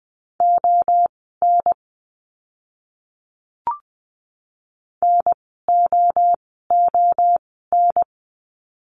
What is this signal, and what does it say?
Morse code, heard as a keyed tone, about 20 wpm, spaced as for 14 wpm, 700 Hz. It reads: ON NOON
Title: 036 Lesson 7f - Letter s - Words 20wpm